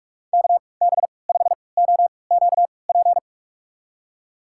Morse code, heard as a keyed tone, about 48 wpm, spaced as for 26 wpm, 700 Hz. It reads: KB5OQP
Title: USA Call Signs - 3x Character Spacing - 45wpm